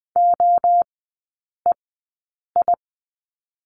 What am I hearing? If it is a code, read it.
Morse code, heard as a keyed tone, about 20 wpm, 700 Hz. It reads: O E I
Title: 029 Lesson 6e - Letter i - Three Characters 20wpm